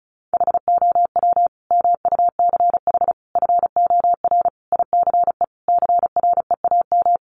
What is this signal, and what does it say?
Morse code, heard as a keyed tone, about 34 wpm, 700 Hz. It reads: HOW MUCH FOR ICE CREAM
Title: Sentences from Top 1500 Words 35wpm